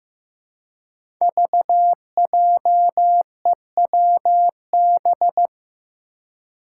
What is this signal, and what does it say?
Morse code, heard as a keyed tone, about 15 wpm, 700 Hz. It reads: VJEWB
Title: Five Challenging Letter-Number Combinations 15wpm